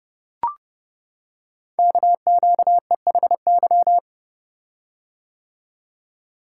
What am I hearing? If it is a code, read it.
Morse code, heard as a keyed tone, about 30 wpm, 700 Hz. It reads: KQEHY
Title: Code Groups - 5 Letters - 30wpm